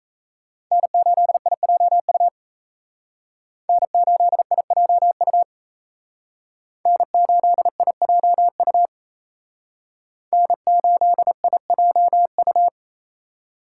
Morse code, heard as a keyed tone, about 36 wpm, 700 Hz. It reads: N8IJU N8IJU N8IJU N8IJU